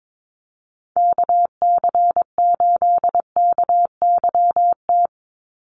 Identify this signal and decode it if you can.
Morse code, heard as a keyed tone, about 22 wpm, 700 Hz. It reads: KC8KYT
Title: USA Call Signs 22wpm